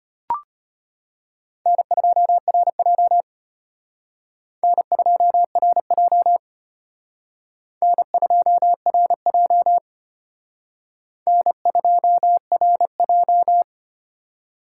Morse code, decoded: N2RJ N2RJ N2RJ N2RJ

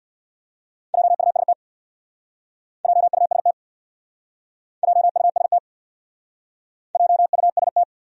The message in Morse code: JUST JUST JUST JUST

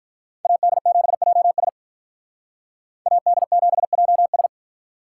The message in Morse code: AD7JS AD7JS

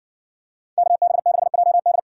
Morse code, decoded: KB6JD

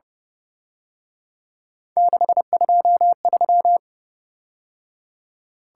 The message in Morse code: 623

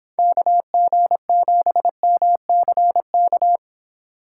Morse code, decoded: KG7MCK

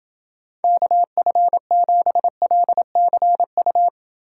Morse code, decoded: KF7LCU

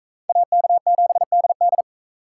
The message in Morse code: AK7DD

